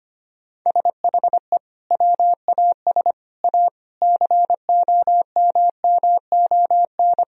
Morse code, heard as a keyed tone, about 25 wpm, 700 Hz. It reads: SHE WAS A COMMON
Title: Sentences from Top 1000 Words - Review Part 4 - 25wpm